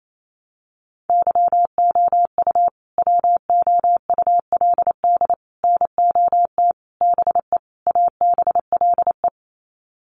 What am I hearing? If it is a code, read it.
Morse code, heard as a keyed tone, about 28 wpm, 700 Hz. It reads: YOU WOULD NOT BE ABLE
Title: Sets of 5 Words Encore2 28wpm